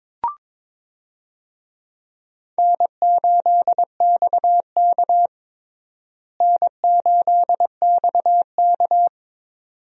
Morse code, heard as a keyed tone, about 22 wpm, 700 Hz. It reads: N8XK N8XK
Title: USA Call Signs - 4 Characters - Repeated 2x - 22wpm